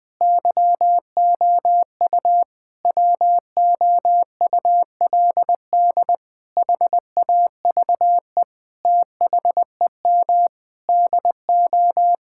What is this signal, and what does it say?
Morse code, heard as a keyed tone, about 20 wpm, 700 Hz. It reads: YOU WOULD HAVE THEM DO
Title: Sentences with 9-Word Phrases - Spoken Beginning - 20wpm